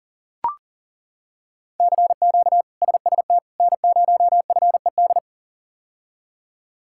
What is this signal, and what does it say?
Morse code, heard as a keyed tone, about 40 wpm, 700 Hz. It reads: CQ SST N0FED
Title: K1USN SST QSO Practice - 40wpm